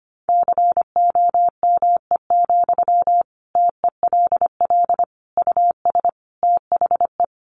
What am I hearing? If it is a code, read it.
Morse code, heard as a keyed tone, about 25 wpm, 700 Hz. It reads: COME, TELL US THE
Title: Sentences from Top 800 Words 25wpm